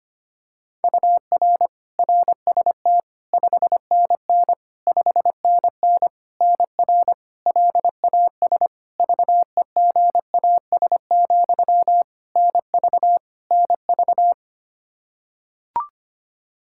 Morse code, heard as a keyed tone, about 25 wpm, 700 Hz. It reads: UR RST 5NN 5NN NR LAS VEGAS, NV NV
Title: K1USN SST QSO Practice 25wpm